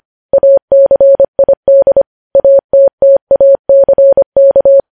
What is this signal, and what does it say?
Morse code, heard as a keyed tone, about 25 wpm, 550 Hz. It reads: ACID ATTACK
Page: news-headlines-in-morse-code-at-25-wpm